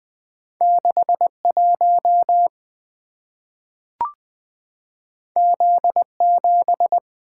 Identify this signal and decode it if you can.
Morse code, heard as a keyed tone, about 20 wpm, 700 Hz. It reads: 61 Z7
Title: Two Letter-Number Combinations - 20wpm